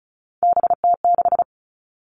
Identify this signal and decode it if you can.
Morse code, heard as a keyed tone, about 34 wpm, 700 Hz. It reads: BT6
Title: All Cut Numbers - 3-Digit 35wpm